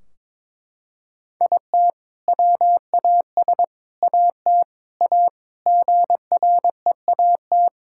Morse code, heard as a keyed tone, about 21 wpm, 700 Hz. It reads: IT WAS AT A GREAT
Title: Sentences from Top 1600 Words 22wpm